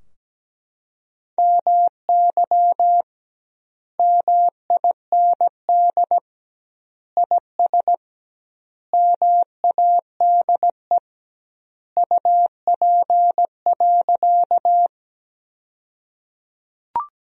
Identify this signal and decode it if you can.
Morse code, heard as a keyed tone, about 17 wpm, 700 Hz. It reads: MY MIND IS MADE UP.
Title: 2x Word-Spacing - Sentences from Top 500 Words - 17wpm